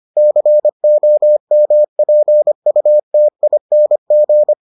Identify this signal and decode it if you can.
Morse code, heard as a keyed tone, about 25 wpm, 600 Hz. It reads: COMPUTING